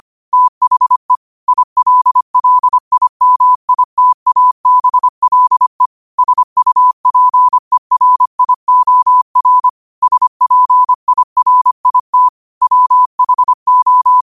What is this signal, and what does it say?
Morse code, heard as a keyed tone, about 25 wpm, 1000 Hz. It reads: THE ILLIMITABLE SUPERIOR SPIRIT WHO
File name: qotdfiles / 25wpm